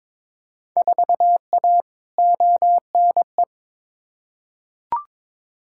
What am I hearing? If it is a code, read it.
Morse code, heard as a keyed tone, about 22 wpm, 700 Hz. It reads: 4A ONE
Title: Field Day Exchanges Encore - 22wpm